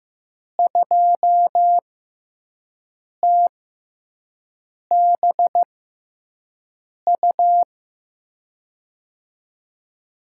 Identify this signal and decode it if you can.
Morse code, heard as a keyed tone, about 15 wpm, spaced as for 5 wpm, 700 Hz. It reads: 2TBU
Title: USA Call Signs - 6 Characters - 6x Character Spacing - 15wpm